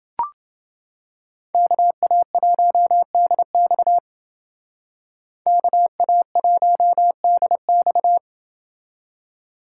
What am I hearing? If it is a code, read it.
Morse code, heard as a keyed tone, about 28 wpm, 700 Hz. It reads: KA1DX KA1DX